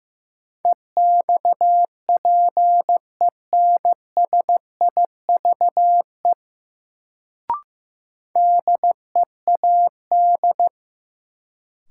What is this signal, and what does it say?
Morse code, heard as a keyed tone, about 15 wpm, 700 Hz. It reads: EXPENSIVE DEAD
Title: Sentences from Top 1500 Words - New Word Drill Alt - 15wpm